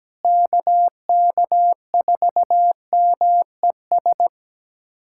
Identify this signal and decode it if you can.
Morse code, heard as a keyed tone, about 17 wpm, 700 Hz. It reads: KK4MES
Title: USA Callsign DE Callsign - 6 Characters - 17wpm